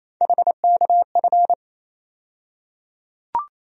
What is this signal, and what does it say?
Morse code, heard as a keyed tone, about 27 wpm, 700 Hz. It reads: HKF
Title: USA Call Sign Postfixes - 3 Letters 28wpm